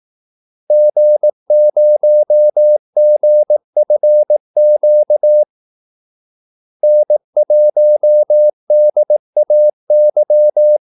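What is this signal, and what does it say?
Morse code, heard as a keyed tone, about 18 wpm, 600 Hz. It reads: G0GFQ N1DAY
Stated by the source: Morse call-sign practice at 18 wpm.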